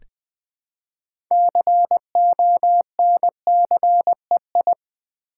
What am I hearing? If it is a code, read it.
Morse code, heard as a keyed tone, about 20 wpm, 700 Hz. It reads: CONCEI
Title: Build A Word - 3 Letters At A Time - 8-Letter Words - 20wpm